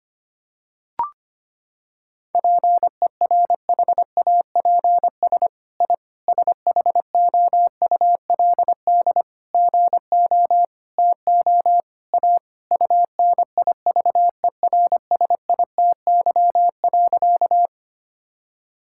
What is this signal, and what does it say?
Morse code, heard as a keyed tone, about 25 wpm, 700 Hz. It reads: PERHAPS I SHOULD GO TO A UNIVERSITY.
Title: Sentences from Top 1000 Words - Review Part 5 - 25wpm